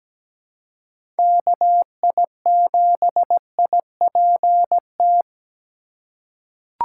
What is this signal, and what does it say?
Morse code, heard as a keyed tone, about 17 wpm, 700 Hz. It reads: KI7IPT
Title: USA Call Signs - 6 Characters - 2x Character Spacing - 17wpm